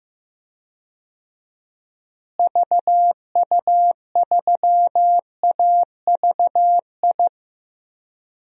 Morse code, heard as a keyed tone, about 16 wpm, 700 Hz. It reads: VU3AVI
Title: DXCC-100 Call Signs - 6 Characters - Repeated 2x - 15wpm